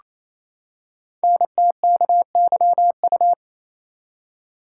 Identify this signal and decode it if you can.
Morse code, heard as a keyed tone, about 29 wpm, 700 Hz. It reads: NTKYU
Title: Code Groups - 5 Letters - 28wpm